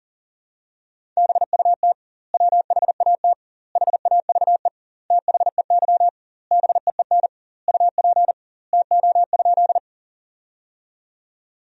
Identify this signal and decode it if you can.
Morse code, heard as a keyed tone, about 40 wpm, 700 Hz. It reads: BUT WHAT HAVE THEY BEEN UP TO?
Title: 2x Word-Spacing - Sentences from Top 100 Words - 40wpm